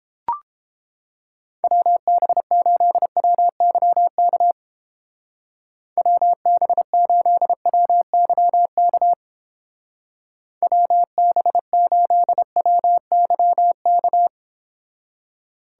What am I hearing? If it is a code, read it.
Morse code, heard as a keyed tone, about 31 wpm, 700 Hz. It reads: WB8WYK WB8WYK WB8WYK